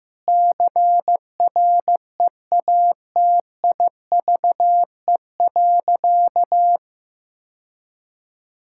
Morse code, heard as a keyed tone, about 16 wpm, 700 Hz. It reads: CREATIVE.
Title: Sentences from Top 1000 Words - Review Part 5 - 15wpm